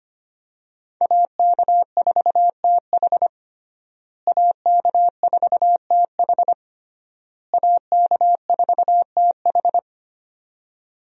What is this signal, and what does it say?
Morse code, heard as a keyed tone, about 25 wpm, 700 Hz. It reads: AK4TH AK4TH AK4TH
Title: USA Call Signs - 5 Characters - Repeated 3x - 25wpm